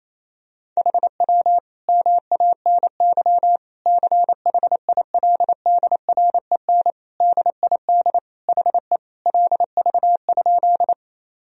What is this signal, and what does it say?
Morse code, heard as a keyed tone, about 27 wpm, 700 Hz. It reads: HW MANY CHILDREN DID HE LV?
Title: Sentences from Top 400 Words and Abbreviations 28wpm